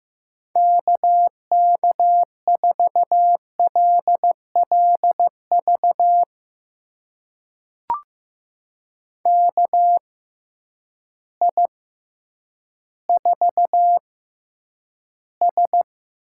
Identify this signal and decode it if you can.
Morse code, heard as a keyed tone, about 15 wpm, spaced as for 9 wpm, 700 Hz. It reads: KK4LLV KI4S
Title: USA Call Signs - 6 Characters - 6x Character Spacing - 15wpm